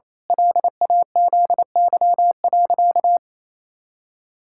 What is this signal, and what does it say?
Morse code, heard as a keyed tone, about 28 wpm, 700 Hz. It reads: LAZY.
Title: Pangram Challenge v2 - 28wpm